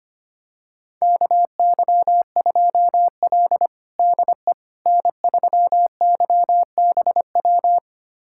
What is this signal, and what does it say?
Morse code, heard as a keyed tone, about 25 wpm, 700 Hz. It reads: KY2L DE N3YBW
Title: USA Callsign DE Callsign 25wpm